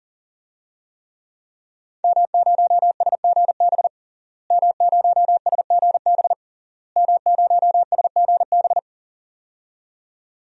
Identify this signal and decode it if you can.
Morse code, heard as a keyed tone, about 37 wpm, 700 Hz. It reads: M0SGB M0SGB M0SGB
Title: DXCC-100 Call Signs - 5 Characters - Repeated 3x - 40wpm